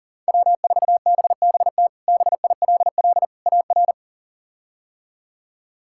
Morse code, heard as a keyed tone, about 40 wpm, 700 Hz. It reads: W4BBT BILL AR
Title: K1USN SST QSO Practice - 40wpm